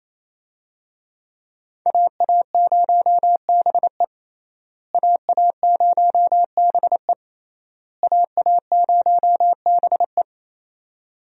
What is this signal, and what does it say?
Morse code, heard as a keyed tone, about 28 wpm, 700 Hz. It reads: AA0BE AA0BE AA0BE
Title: USA Call Signs - 5 Characters - Repeated 3x - 28wpm